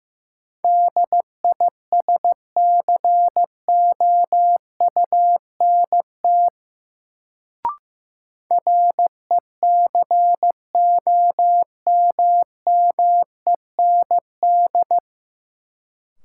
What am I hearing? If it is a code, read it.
Morse code, heard as a keyed tone, about 15 wpm, 700 Hz. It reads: DISCOUNT RECOMMEND
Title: Sentences from Top 1200 Words - New Word Drill - 15wpm